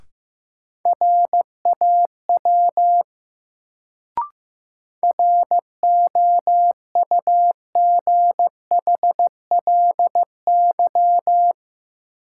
Morse code, heard as a keyed tone, about 15 wpm, 700 Hz. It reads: RAW ROUGHLY
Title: Sentences from Top 1500 Words - New Word Drill Alt - 15wpm